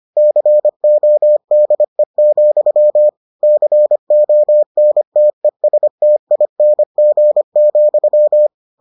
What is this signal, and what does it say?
Morse code, heard as a keyed tone, about 25 wpm, 600 Hz. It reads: CODE, CONTESTING,